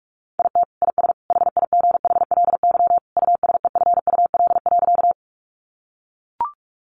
Morse code, heard as a keyed tone, about 44 wpm, 700 Hz. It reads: IT IS HIGHLY USEFUL.
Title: Sentences from Top 1100 Words 45wpm